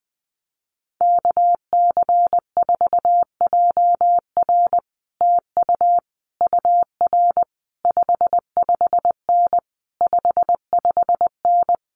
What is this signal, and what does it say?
Morse code, heard as a keyed tone, about 20 wpm, 700 Hz. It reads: KC4JR TU UR 55N 55N
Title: Common QSO Phrases Encore 20wpm